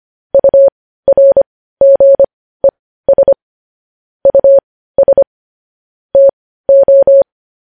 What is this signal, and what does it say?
Morse code, heard as a keyed tone, about 25 wpm, spaced as for 15 wpm, 550 Hz. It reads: URGES US TO
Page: news-headlines-in-morse-code-at-15-wpm